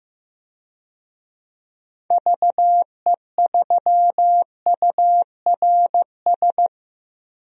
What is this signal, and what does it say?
Morse code, heard as a keyed tone, about 15 wpm, 700 Hz. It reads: VE3URS
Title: DXCC-100 Call Signs - 6 Characters - 15wpm